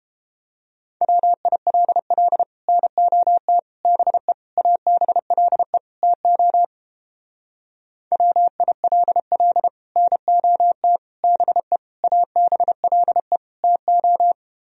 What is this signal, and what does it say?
Morse code, heard as a keyed tone, about 29 wpm, 700 Hz. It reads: WILL NOT BE ABLE TO WILL NOT BE ABLE TO